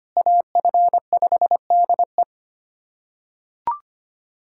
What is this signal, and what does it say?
Morse code, heard as a keyed tone, about 25 wpm, 700 Hz. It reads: AF5DE
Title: USA Callsign DE Callsign - 5 Characters - 25wpm